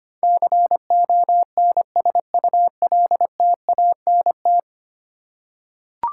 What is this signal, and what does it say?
Morse code, heard as a keyed tone, about 25 wpm, 700 Hz. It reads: CONSULTANT
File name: Ten Letter Words - 25wpm